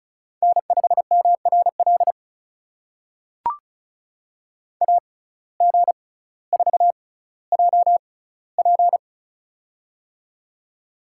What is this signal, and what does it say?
Morse code, heard as a keyed tone, about 36 wpm, spaced as for 18 wpm, 700 Hz. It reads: N5MRL AG4JP
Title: USA Call Signs - 5 Characters - 6x Character Spacing - 35wpm